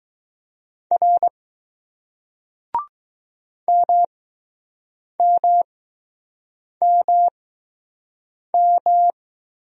Morse code, heard as a keyed tone, about 20 wpm, spaced as for 6 wpm, 700 Hz. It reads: R MMMM